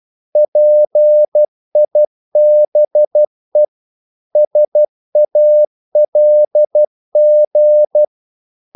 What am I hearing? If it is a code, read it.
Morse code, heard as a keyed tone, about 12 wpm, 600 Hz. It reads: PIBE SALG
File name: Korte ord DK 12wpm